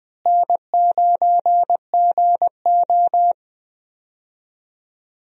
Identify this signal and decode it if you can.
Morse code, heard as a keyed tone, about 20 wpm, 700 Hz. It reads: N9GO
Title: USA Call Signs - 4 Characters - Repeated 3x - 20wpm